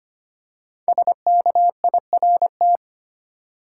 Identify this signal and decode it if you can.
Morse code, heard as a keyed tone, about 25 wpm, 700 Hz. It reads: SKIRT
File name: 5 Letter Words - First 3 Letters Given - 25wpm